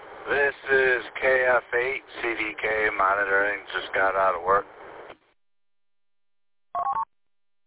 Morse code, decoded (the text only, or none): none